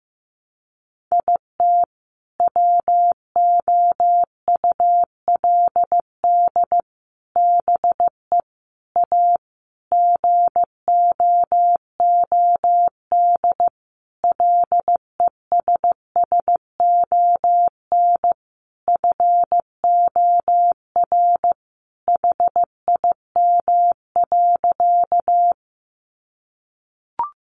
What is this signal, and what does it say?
Morse code, heard as a keyed tone, about 15 wpm, 700 Hz. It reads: IT WOULD BE A GOOD LESSON FOR HIM.
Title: Sentences from Top 1200 Words 15wpm